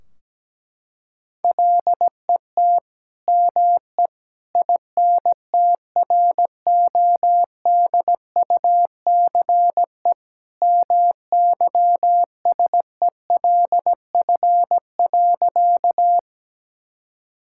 Morse code, heard as a keyed tone, about 17 wpm, 700 Hz. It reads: LET ME INTRODUCE MYSELF.
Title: Sentences from Top 1600 Words 17wpm